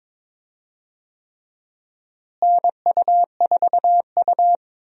Morse code, heard as a keyed tone, about 22 wpm, 700 Hz. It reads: NU4U
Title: USA Call Signs - 4 Characters - Repeated 3x - 22wpm